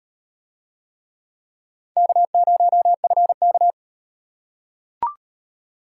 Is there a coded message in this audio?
Yes: Morse code from 2.0 to 3.7 s, heard as a keyed tone, about 38 wpm, 700 Hz. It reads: K0FK